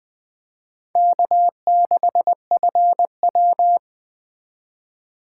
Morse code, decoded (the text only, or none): K6FW